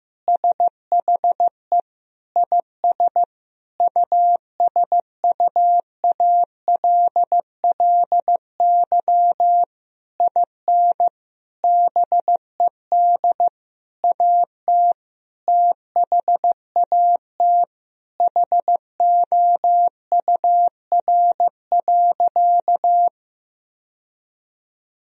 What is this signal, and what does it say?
Morse code, heard as a keyed tone, about 15 wpm, 700 Hz. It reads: SHE IS USUALLY IN BED AT THAT HOUR.
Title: Sentences from Top 1000 Words - Review Part 3 - 15wpm